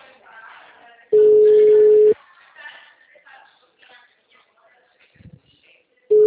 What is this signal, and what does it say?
Ringback tone (Europe, 425 Hz), with speech.